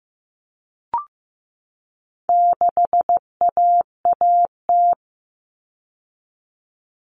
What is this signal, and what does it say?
Morse code, heard as a keyed tone, about 15 wpm, 700 Hz. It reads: 6AAT
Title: Common Cut Numbers - 4-Digit 15wpm